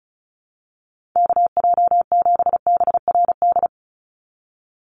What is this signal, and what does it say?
Morse code, heard as a keyed tone, about 34 wpm, 700 Hz. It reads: KJ7BRD
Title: USA Call Signs 35wpm